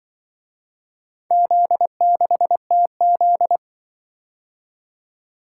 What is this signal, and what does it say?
Morse code, heard as a keyed tone, about 24 wpm, 700 Hz. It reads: Z6TZ
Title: Speed-Racing - Four Letter-Number Combinations - 20wpm